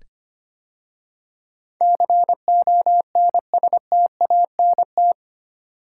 Morse code, heard as a keyed tone, about 25 wpm, 700 Hz. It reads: CONSTANT
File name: 8 Letter Words - First 3 Letters Given - 25wpm